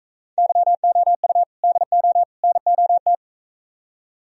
Morse code, heard as a keyed tone, about 43 wpm, 700 Hz. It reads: YOU DO NOT